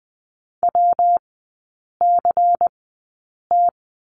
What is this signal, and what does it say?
Morse code, heard as a keyed tone, about 20 wpm, spaced as for 8 wpm, 690 Hz. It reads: WCT
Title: 166 Lesson 27e - Number 7 - Three Characters 20wpm